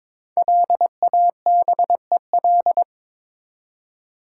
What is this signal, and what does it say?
Morse code, heard as a keyed tone, about 22 wpm, 700 Hz. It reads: LABEL
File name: 5 Letter Words - First 3 Letters Given - 22wpm